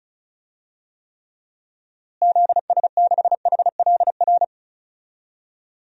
Morse code, heard as a keyed tone, about 36 wpm, 700 Hz. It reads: ZS6HLR
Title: DXCC-100 Call Signs - 6 Characters - Repeated 3x - 35wpm